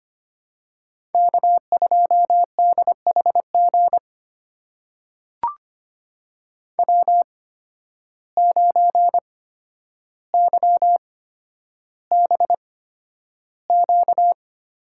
Morse code, heard as a keyed tone, about 25 wpm, spaced as for 11 wpm, 700 Hz. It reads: K2DHG W9YBQ
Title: USA Call Signs - 5 Characters - 8x Character Spacing - 25wpm